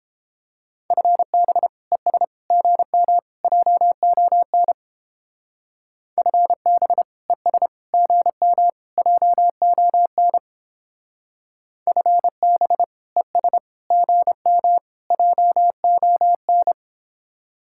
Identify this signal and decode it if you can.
Morse code, heard as a keyed tone, about 30 wpm, 700 Hz. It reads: FB ES GM JON FB ES GM JON FB ES GM JON